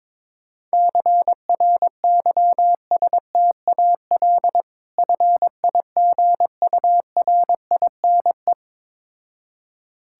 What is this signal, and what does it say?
Morse code, heard as a keyed tone, about 22 wpm, 690 Hz. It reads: CRYSTAL FIGURINE
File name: Difficult Words with Hints - 22wpm